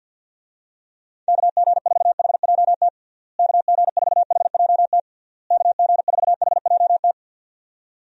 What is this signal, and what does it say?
Morse code, heard as a keyed tone, about 52 wpm, 700 Hz. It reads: KG4HJT KG4HJT KG4HJT
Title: USA Call Signs - 6 Characters - Repeated 3x - 50wpm